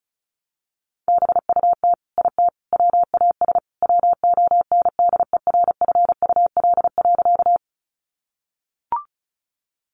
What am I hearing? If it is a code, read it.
Morse code, heard as a keyed tone, about 34 wpm, 700 Hz. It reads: BUT IT WAS WONDERFUL.
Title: Sentences from Top 1100 Words 35wpm